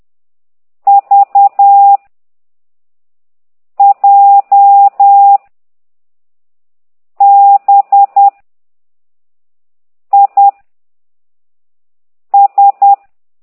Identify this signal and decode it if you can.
Morse code, heard as a keyed tone, about 10 wpm, 800 Hz. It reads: V J B I S
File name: ct2idw / exerc9